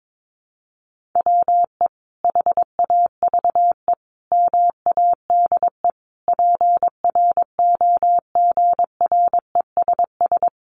Morse code, heard as a keyed tone, about 22 wpm, 700 Hz. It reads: WE HAVE MADE PROGRESS
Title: Sentences from Top 1400 Words 22wpm